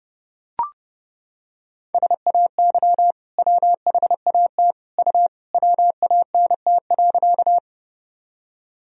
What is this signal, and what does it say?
Morse code, heard as a keyed tone, about 32 wpm, 700 Hz. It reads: SAY WHAT U WANT.
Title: Sentences from Top 200 Words and Abbreviations 30wpm